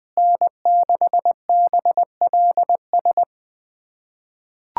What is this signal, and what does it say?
Morse code, heard as a keyed tone, about 20 wpm, 700 Hz. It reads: N6BLS
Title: USA Callsign DE Callsign - 5 Characters - 20wpm